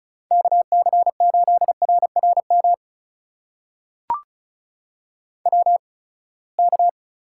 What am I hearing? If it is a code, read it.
Morse code, heard as a keyed tone, about 36 wpm, spaced as for 24 wpm, 700 Hz. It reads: KC8RRM WK
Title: USA Call Signs - 8x Character Spacing - 35wpm